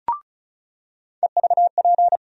Morse code, heard as a keyed tone, about 36 wpm, 700 Hz. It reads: EVP
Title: USA Call Sign Postfixes - 3 Letters 35wpm